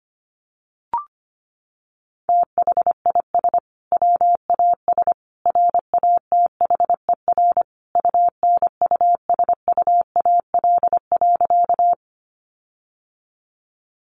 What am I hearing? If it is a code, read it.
Morse code, heard as a keyed tone, about 25 wpm, 700 Hz. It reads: THIS WAS RATHER UNUSUAL.
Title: Sentences from Top 1400 Words 25wpm